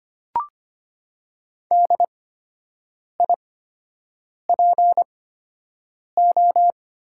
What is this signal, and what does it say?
Morse code, heard as a keyed tone, about 25 wpm, spaced as for 7 wpm, 700 Hz. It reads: DIPO
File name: Top 90 QSO Elements - 8x Character Spacing - 25wpm